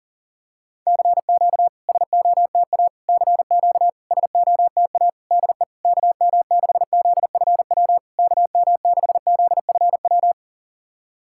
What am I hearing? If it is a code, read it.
Morse code, heard as a keyed tone, about 42 wpm, 660 Hz. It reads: CQ SOTA CQ SOTA DE KM6ZFW KM6ZFW
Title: SOTA QSO Practice 40wpm